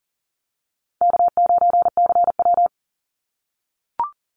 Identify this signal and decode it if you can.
Morse code, heard as a keyed tone, about 41 wpm, 700 Hz. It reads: K9CW
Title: Easy Call Signs - 40wpm